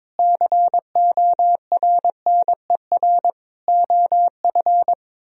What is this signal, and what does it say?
Morse code, heard as a keyed tone, about 22 wpm, 700 Hz. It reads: CORNER OF
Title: Speed-Racing - Sets of 5 Words 22wpm